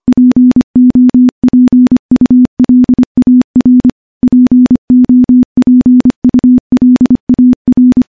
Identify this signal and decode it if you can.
Morse code, heard as a keyed tone, about 25 wpm, 250 Hz. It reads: POPULAR POPULAR
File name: morse-training-25WPM